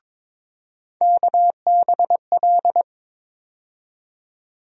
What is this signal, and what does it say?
Morse code, heard as a keyed tone, about 22 wpm, 700 Hz. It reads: KBL
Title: USA Call Sign Postfixes - 3 Letters 22wpm